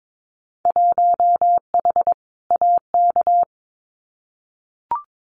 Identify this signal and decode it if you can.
Morse code, heard as a keyed tone, about 22 wpm, 700 Hz. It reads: 1H AK
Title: Winter Field Day Exchanges Encore 22wpm